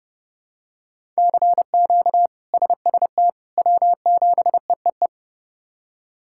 Morse code, heard as a keyed tone, about 31 wpm, 700 Hz. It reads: CQ SST W7EEE
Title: K1USN SST QSO Practice - 30wpm